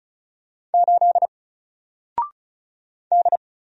Morse code, heard as a keyed tone, about 36 wpm, 700 Hz. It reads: 8 D
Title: Single Letter-Number Alt 35wpm